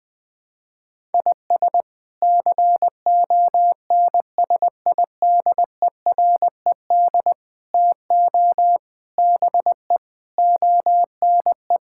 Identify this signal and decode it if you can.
Morse code, heard as a keyed tone, about 21 wpm, 700 Hz. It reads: IS CONSIDERED TO BE ONE
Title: Sets of 8 Words - 20wpm